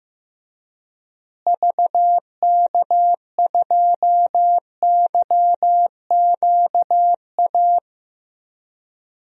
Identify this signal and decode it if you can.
Morse code, heard as a keyed tone, about 15 wpm, 700 Hz. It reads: VK2YQA
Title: DXCC-100 Call Signs - 6 Characters - Repeated 2x - 15wpm